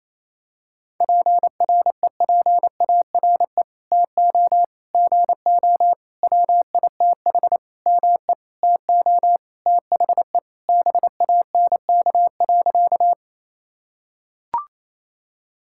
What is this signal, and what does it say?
Morse code, heard as a keyed tone, about 28 wpm, 700 Hz. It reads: PREPARE TO GO WITH ME TO THE BANK.
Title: Sentences from Top 1000 Words - Review Part 6 - 28wpm